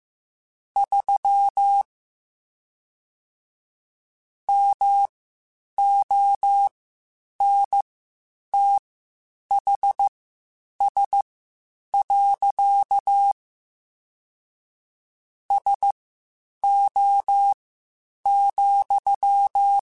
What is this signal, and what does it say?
Morse code, heard as a keyed tone, about 15 wpm, spaced as for 8 wpm, 800 Hz. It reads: 3 MONTHS. SO,
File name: (spaced) / QSOs Spaced-mp3s / WD0BGP 1 @13 sp